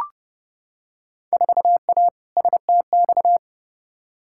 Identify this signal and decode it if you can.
Morse code, heard as a keyed tone, about 32 wpm, 700 Hz. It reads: 4A STX
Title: Field Day Exchanges - 30wpm